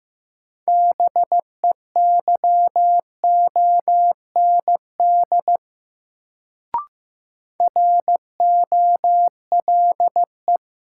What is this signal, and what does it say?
Morse code, heard as a keyed tone, about 16 wpm, 700 Hz. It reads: BEYOND ROLE
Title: Sentences from Top 1000 Words - Review Part 6 - New Word Drill - 15wpm